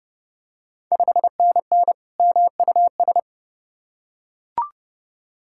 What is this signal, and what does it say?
Morse code, heard as a keyed tone, about 32 wpm, 700 Hz. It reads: 5NN MUS
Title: Ontario QSO Party Exchange 30wpm